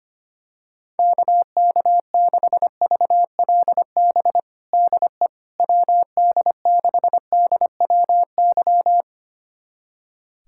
Morse code, heard as a keyed tone, about 25 wpm, 700 Hz. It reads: KK6VLB DE WD6DWY
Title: USA Callsign DE Callsign - 6 Characters - 25wpm